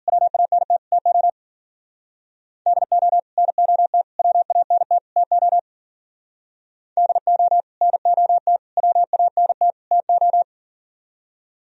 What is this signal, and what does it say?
Morse code, heard as a keyed tone, about 49 wpm, 700 Hz. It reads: WANT TO DO NOT WANT TO DO NOT WANT TO
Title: Speed-Racing - Sets of 4 Words 40wpm